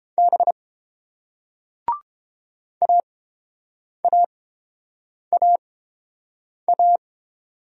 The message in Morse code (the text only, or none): B AAAA